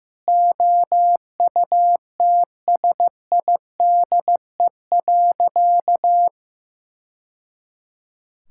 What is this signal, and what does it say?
Morse code, heard as a keyed tone, about 14 wpm, 700 Hz. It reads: OUTSIDE.
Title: Sentences from Top 1000 Words - Review Part 6 - 15wpm